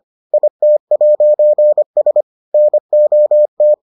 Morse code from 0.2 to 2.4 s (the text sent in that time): IT'S